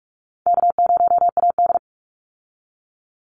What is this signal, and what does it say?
Morse code, heard as a keyed tone, about 44 wpm, 700 Hz. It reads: K0AD
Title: Call Signs Repeated 3x - 45wpm